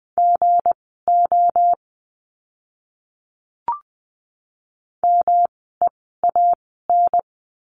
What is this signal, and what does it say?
Morse code, heard as a keyed tone, about 20 wpm, spaced as for 14 wpm, 700 Hz. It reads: GO MEAN
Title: 237 Lesson 38f - Letter q - Words 20wpm